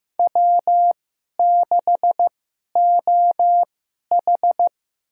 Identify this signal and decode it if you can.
Morse code, heard as a keyed tone, about 15 wpm, spaced as for 11 wpm, 700 Hz. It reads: W6OH
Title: USA Call Signs - 4 Characters - 2x Character Spacing - 15wpm